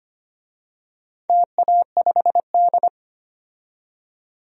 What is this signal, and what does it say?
Morse code, heard as a keyed tone, about 25 wpm, 700 Hz. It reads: TA5D
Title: DXCC-100 Call Signs - 4 Characters - 25wpm